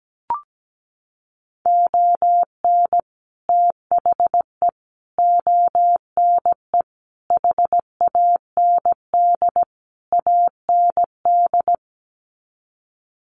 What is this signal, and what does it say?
Morse code, heard as a keyed tone, about 17 wpm, 700 Hz. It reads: ON THE ONE HAND AND
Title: Sets of 5 Words Encore2 17wpm